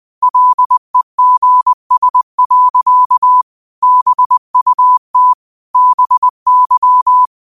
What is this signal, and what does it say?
Morse code, heard as a keyed tone, about 20 wpm, 1000 Hz. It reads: LEGS. BUT BY
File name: qotdfiles / 20wpm